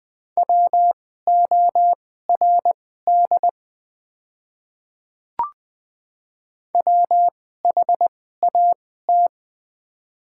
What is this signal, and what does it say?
Morse code, heard as a keyed tone, about 21 wpm, spaced as for 15 wpm, 700 Hz. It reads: WORD WHAT
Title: 2x Character Spacing - Top 300 Words - 20wpm